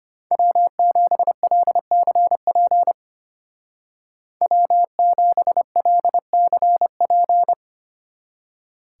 Morse code, heard as a keyed tone, about 33 wpm, 700 Hz. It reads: W7LCP W7LCP